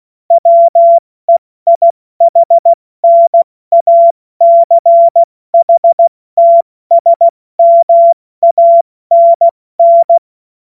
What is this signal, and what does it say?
Morse code, heard as a keyed tone, about 16 wpm, 680 Hz. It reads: WEIHNACHTSMANN
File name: Morsekurs / Witze / Witz 09. November 2021, 70 F80